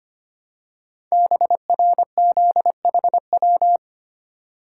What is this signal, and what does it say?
Morse code, heard as a keyed tone, about 25 wpm, 700 Hz. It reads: BRZHW
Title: Code Groups - 5 Letters - 25wpm